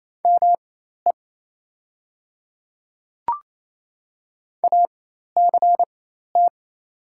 Morse code, heard as a keyed tone, about 27 wpm, spaced as for 13 wpm, 700 Hz. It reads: ME ACT
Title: 4x Character Spacing - Top 200 Words - 28wpm